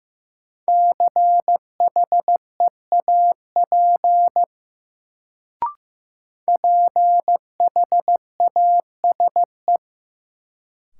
Morse code, heard as a keyed tone, about 16 wpm, 700 Hz. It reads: CHEAP PHASE
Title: Sentences from Top 1200 Words - New Word Drill - 15wpm